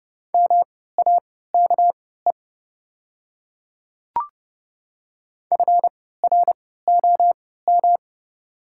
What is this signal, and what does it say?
Morse code, heard as a keyed tone, about 30 wpm, spaced as for 17 wpm, 690 Hz. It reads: MAKE FROM
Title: 3x Character Spacing - Top 100 Words - 30wpm